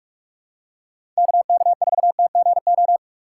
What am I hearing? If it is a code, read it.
Morse code, heard as a keyed tone, about 47 wpm, 700 Hz. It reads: KK4TGO
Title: USA Call Signs - 6 Characters - 3x Character Spacing - 45wpm